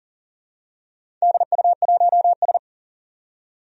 Morse code, heard as a keyed tone, about 40 wpm, 700 Hz. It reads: DU1S
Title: DXCC-100 Call Signs - 4 Characters - 40wpm